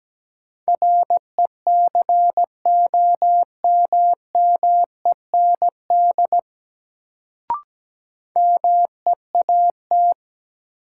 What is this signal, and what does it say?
Morse code, heard as a keyed tone, about 17 wpm, 700 Hz. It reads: RECOMMEND MEAT
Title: Sentences from Top 1200 Words - New Word Drill Alt - 17wpm